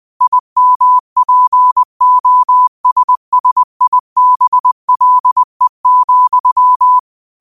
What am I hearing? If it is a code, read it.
Morse code, heard as a keyed tone, about 20 wpm, 1000 Hz. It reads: IMPOSSIBLE,